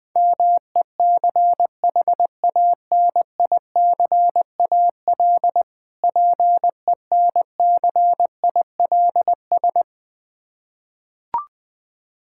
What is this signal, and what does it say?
Morse code, heard as a keyed tone, about 20 wpm, 690 Hz. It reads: MECHANICAL PENCILS
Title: Difficult Words with Hints - 20wpm